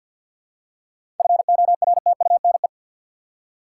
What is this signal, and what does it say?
Morse code, heard as a keyed tone, about 50 wpm, 700 Hz. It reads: FORTUNE
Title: Seven Letter Words - 50wpm